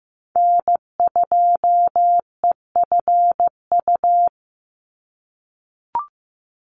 Morse code, heard as a keyed tone, about 15 wpm, 700 Hz. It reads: N2EFU
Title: USA Call Signs 15wpm